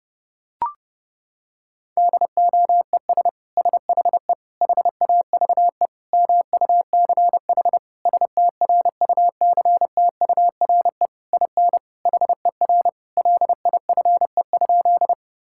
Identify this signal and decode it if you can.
Morse code, heard as a keyed tone, about 32 wpm, 700 Hz. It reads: DOES SHE HAVE MUCH STRUCTURE IN HER LIFE?
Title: Sentences from Top 1000 Words - Review Part 6 - 30wpm